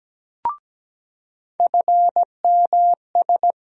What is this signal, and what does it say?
Morse code, heard as a keyed tone, about 17 wpm, 700 Hz. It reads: FMS
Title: USA Call Sign Postfixes - 3 Letters 17wpm